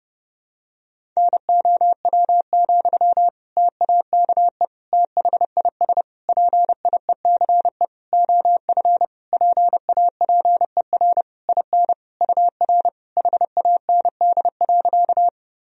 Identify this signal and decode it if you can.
Morse code, heard as a keyed tone, about 30 wpm, 700 Hz. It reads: NOW, TAKE THIS PIECE OF PAPER IN UR HAND.
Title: Sentences from Top 400 Words and Abbreviations 30wpm